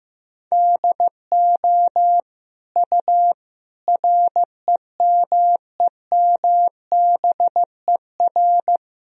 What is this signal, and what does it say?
Morse code, heard as a keyed tone, about 15 wpm, 700 Hz. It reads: DO U REMEMBER
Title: Sentences from Top 400 Words and Abbreviations 15wpm